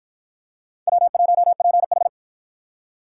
Morse code, heard as a keyed tone, about 47 wpm, 700 Hz. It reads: W1PH